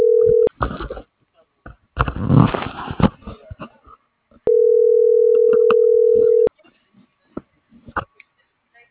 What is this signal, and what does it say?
Ringback tone (North America), 440 + 480 Hz.